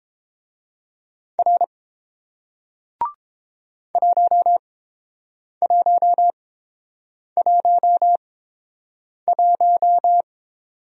Morse code, heard as a keyed tone, about 29 wpm, spaced as for 7 wpm, 700 Hz. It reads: R 1111